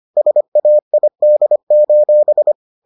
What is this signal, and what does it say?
Morse code, heard as a keyed tone, about 25 wpm, 600 Hz. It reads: SAID: